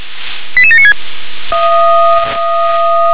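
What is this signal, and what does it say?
Five-tone selective call (ZVEI), heard as quick runs of five tones, tones of about 70 ms. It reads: 88796
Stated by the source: Received on an SDR; ZVEI five-tone selective call.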